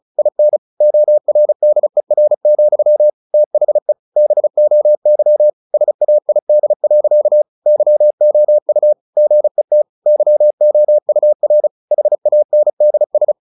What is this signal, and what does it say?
Morse code, heard as a keyed tone, about 34 wpm, 600 Hz. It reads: IN ORDER, THE BOY SAID. YOU GET YOUR HANDS
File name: the old man and the sea 2 35wpm